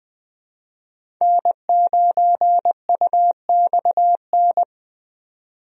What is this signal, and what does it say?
Morse code, heard as a keyed tone, about 21 wpm, 700 Hz. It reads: N9UXN
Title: USA Call Signs - 5 Characters - 20wpm